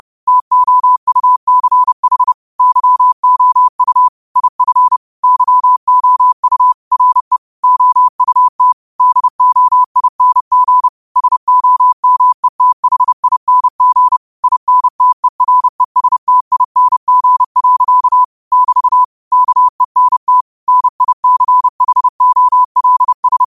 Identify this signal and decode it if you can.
Morse code, heard as a keyed tone, about 30 wpm, 1000 Hz. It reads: TOUCH YOU IF YOU RE OUT DOING SOMETHING INTERESTING. <BT> KENT NICHOLS
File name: 30wpm